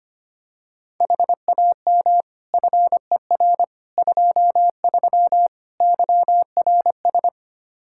Morse code, heard as a keyed tone, about 25 wpm, 700 Hz. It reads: HAM FER 23 YRS
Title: Practice QSO - Exchange 3 - More Info - 25wpm